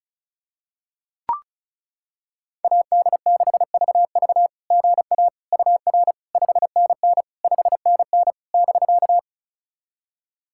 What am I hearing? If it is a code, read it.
Morse code, heard as a keyed tone, about 34 wpm, 690 Hz. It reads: AD6VV GA UR 5NN 5NN <BK>
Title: SOTA QSO Practice 35wpm